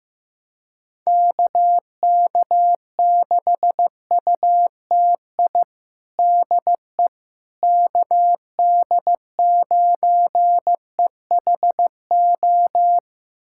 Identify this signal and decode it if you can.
Morse code, heard as a keyed tone, about 15 wpm, 700 Hz. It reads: KK6UTI DE KD9EHO
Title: USA Callsign DE Callsign 15wpm